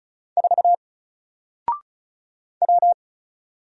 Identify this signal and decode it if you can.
Morse code, heard as a keyed tone, about 36 wpm, 700 Hz. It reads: 4 W
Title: Single Letter-Number Alt 35wpm